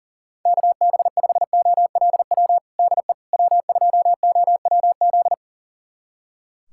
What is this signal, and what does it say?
Morse code, heard as a keyed tone, about 40 wpm, 700 Hz. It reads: KB5OLW DE W2OWZ
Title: USA Callsign DE Callsign 40wpm